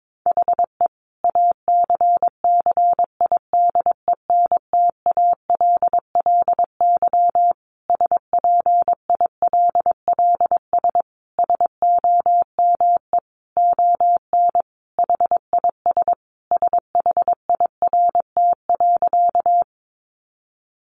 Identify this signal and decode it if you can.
Morse code, heard as a keyed tone, about 22 wpm, 700 Hz. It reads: HE ACCIDENTALLY SPILLS SOME ON HIS SHIRT.
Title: Additional Easy Phrases 22wpm